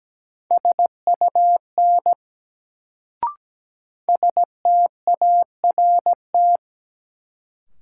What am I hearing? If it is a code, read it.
Morse code, heard as a keyed tone, about 17 wpm, 700 Hz. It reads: SUN START
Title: Sentences from Top 300 Words - New Word Drill - 17wpm